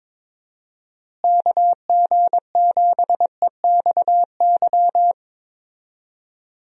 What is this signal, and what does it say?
Morse code, heard as a keyed tone, about 22 wpm, 700 Hz. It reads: KG7EXY
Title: USA Call Signs - 6 Characters - Repeated 2x - 22wpm